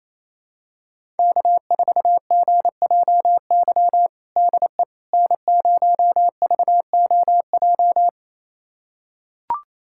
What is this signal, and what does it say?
Morse code, heard as a keyed tone, about 28 wpm, 700 Hz. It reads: K4GJY DE N0VOJ
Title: USA Callsign DE Callsign - 5 Characters - 28wpm